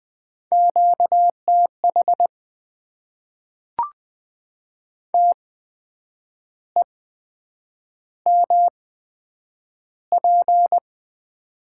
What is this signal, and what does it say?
Morse code, heard as a keyed tone, about 20 wpm, spaced as for 8 wpm, 700 Hz. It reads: QTH TEMP
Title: Top 90 QSO Elements - 8x Character Spacing - 20wpm